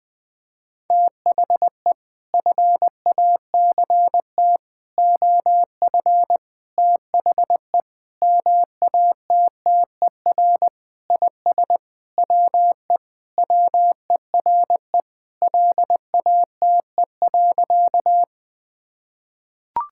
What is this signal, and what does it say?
Morse code, heard as a keyed tone, about 21 wpm, 700 Hz. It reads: THE FACT OF THE MATTER IS WE WERE LATE.
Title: Sentences with 5-Word Phrases - 20wpm